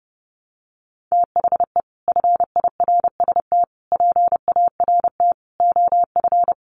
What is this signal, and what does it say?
Morse code, heard as a keyed tone, about 30 wpm, 700 Hz. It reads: THE FIRST PART OF
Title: Sets of 5 Words Encore 30wpm